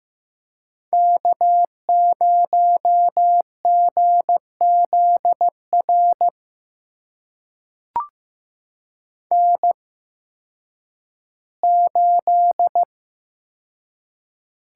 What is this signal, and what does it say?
Morse code, heard as a keyed tone, about 15 wpm, spaced as for 10 wpm, 700 Hz. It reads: K0GZR N8
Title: USA Call Signs - 8x Character Spacing - 15wpm